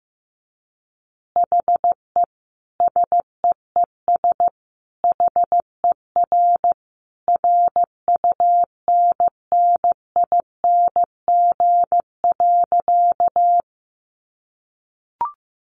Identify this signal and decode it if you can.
Morse code, heard as a keyed tone, about 15 wpm, 700 Hz. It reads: HE SEES HER RUNNING.
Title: Additional Easy Phrases 15wpm